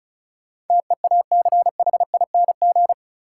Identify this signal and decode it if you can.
Morse code, heard as a keyed tone, about 36 wpm, 700 Hz. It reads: TEACHING
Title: Sentences from Top 1100 Words - New Word Drill - 35wpm